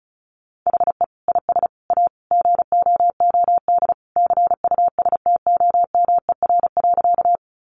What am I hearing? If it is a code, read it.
Morse code, heard as a keyed tone, about 34 wpm, 700 Hz. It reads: HE IS A GOOD CUSTOMER.
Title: Sentences from Top 1100 Words 35wpm